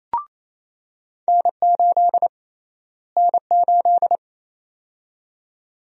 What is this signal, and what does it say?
Morse code, heard as a keyed tone, about 28 wpm, 700 Hz. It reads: N8 N8
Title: USA Call Sign Prefixes - Repeated 2x - 28wpm